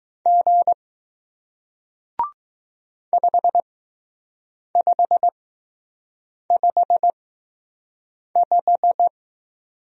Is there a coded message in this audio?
Yes: Morse code from 0.3 to 9.1 s, heard as a keyed tone, about 20 wpm, spaced as for 6 wpm, 700 Hz. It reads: G 5555